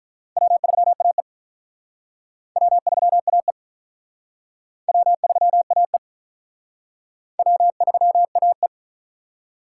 Morse code, heard as a keyed tone, about 43 wpm, 700 Hz. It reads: W3AE W3AE W3AE W3AE
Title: Speed Racing - USA 1x2 Call Signs 35wpm